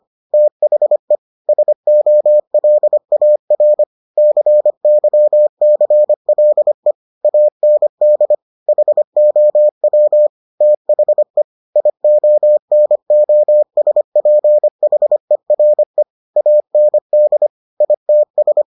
Morse code, heard as a keyed tone, about 25 wpm, 600 Hz. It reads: THE SOLAR CYCLE AND HOW THE IONOSPHERE AND ITS